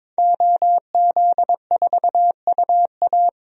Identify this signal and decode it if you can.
Morse code, heard as a keyed tone, about 22 wpm, 700 Hz. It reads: OZ4UA
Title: DXCC-100 Call Signs - 5 Characters - Repeated 2x - 22wpm